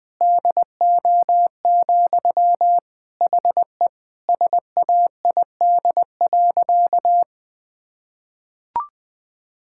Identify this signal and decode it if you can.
Morse code, heard as a keyed tone, about 21 wpm, 700 Hz. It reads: DO, HE SAID.
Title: Sentences from Top 1600 Words 20wpm